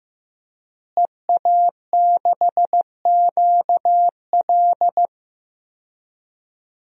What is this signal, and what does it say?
Morse code, heard as a keyed tone, about 15 wpm, 700 Hz. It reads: EA6QL
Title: DXCC-100 Call Signs - 5 Characters - Repeated 2x - 15wpm